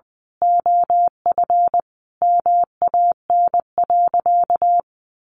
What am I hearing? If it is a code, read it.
Morse code, heard as a keyed tone, about 20 wpm, 700 Hz. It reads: OF MAN.
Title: Sentences from Top 1400 Words 20wpm